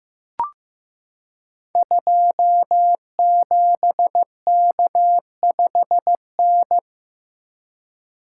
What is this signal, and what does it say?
Morse code, heard as a keyed tone, about 15 wpm, 700 Hz. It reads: 27K5N
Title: Code Groups - 5 Letter:Numbers - 15wpm